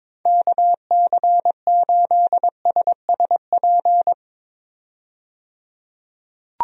Morse code, heard as a keyed tone, about 22 wpm, 700 Hz. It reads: KC8SSP
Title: USA Call Signs - 6 Characters - 22wpm